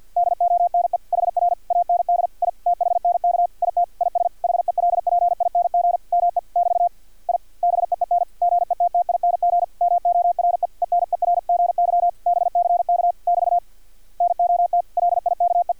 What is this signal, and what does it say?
Morse code, heard as a keyed tone, about 50 wpm, 700 Hz. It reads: DONE HR AND I THNK IT IS HELPING ME <BT> I BEEN GETTING MORE ENERGY BCK <BT> NOT LIKE
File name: archived / CRSnet